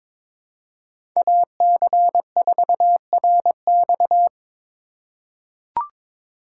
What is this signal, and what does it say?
Morse code, heard as a keyed tone, about 22 wpm, 700 Hz. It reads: AC4RX
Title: USA Call Signs - 3x Character Spacing - 22wpm